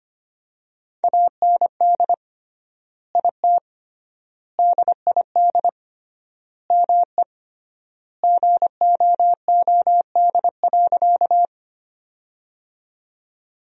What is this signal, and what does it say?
Morse code, heard as a keyed tone, about 25 wpm, 700 Hz. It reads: AND IT DID ME GOOD.
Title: 3x Word-Spacing - Sentences from Top 200 Words - 25wpm